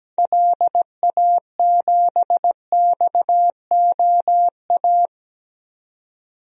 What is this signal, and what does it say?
Morse code, heard as a keyed tone, about 17 wpm, 700 Hz. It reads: LA7XOA
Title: DXCC-100 Call Signs - 6 Characters - 17wpm